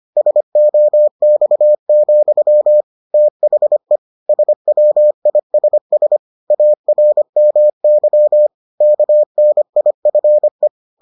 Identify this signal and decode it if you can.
Morse code, heard as a keyed tone, about 25 wpm, 600 Hz. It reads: SOX, THE SWISS ARMY KNIFE